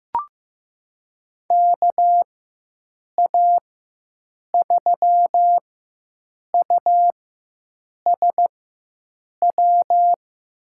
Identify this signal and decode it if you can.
Morse code, heard as a keyed tone, about 15 wpm, spaced as for 7 wpm, 700 Hz. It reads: KA3USW
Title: USA Call Signs - 6 Characters - 4x Character Spacing - 15wpm